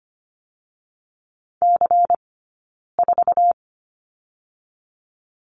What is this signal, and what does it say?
Morse code, heard as a keyed tone, about 25 wpm, 700 Hz. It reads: C 4
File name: course / 25:10wpm / 092 Lesson 16d - Letter c - Two Characters 25wpm